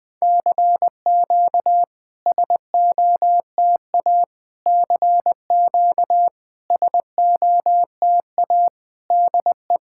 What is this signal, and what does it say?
Morse code, heard as a keyed tone, about 20 wpm, 700 Hz. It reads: CQ SOTA CQ SOTA DE
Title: SOTA QSO Practice 20wpm